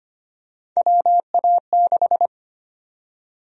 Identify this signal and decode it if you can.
Morse code, heard as a keyed tone, about 25 wpm, 700 Hz. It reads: WA6
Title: USA Call Sign Prefixes - Repeated 2x - 25wpm